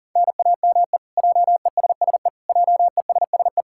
Morse code, heard as a keyed tone, about 38 wpm, 700 Hz. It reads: NAME JESSE JESSE
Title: K1USN SST QSO Practice 40wpm